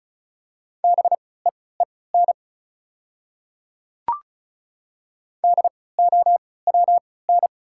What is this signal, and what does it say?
Morse code, heard as a keyed tone, about 36 wpm, spaced as for 20 wpm, 700 Hz. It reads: BEEN DOWN
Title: 3x Character Spacing - Top 200 Words - 35wpm